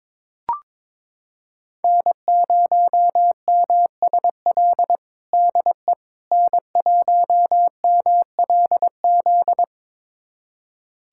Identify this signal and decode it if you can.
Morse code, heard as a keyed tone, about 21 wpm, 700 Hz. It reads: N0MSL DE N1MLZ
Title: USA Callsign DE Callsign - 5 Characters - 22wpm